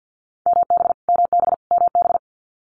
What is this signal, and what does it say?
Morse code, heard as a keyed tone, about 50 wpm, 700 Hz. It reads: MB MB MB
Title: Multi-letter Phonemes Repeated 3x 50wpm